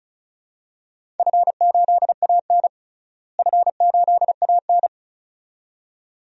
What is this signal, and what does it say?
Morse code, heard as a keyed tone, about 34 wpm, 700 Hz. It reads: F8AN F8AN
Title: DXCC-100 Call Signs - 4 Characters - Repeated 2x - 35wpm